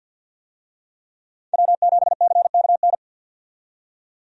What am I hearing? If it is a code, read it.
Morse code, heard as a keyed tone, about 51 wpm, 700 Hz. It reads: W7CKN
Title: USA Call Signs - 5 Characters - 50wpm